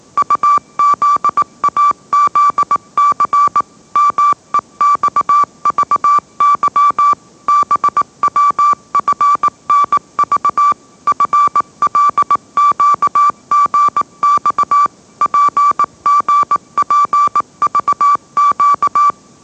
Morse code, heard as a keyed tone, about 18 wpm, 1160 Hz. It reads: UZAZC MEXVY BWFNV FLQGX PGPVQ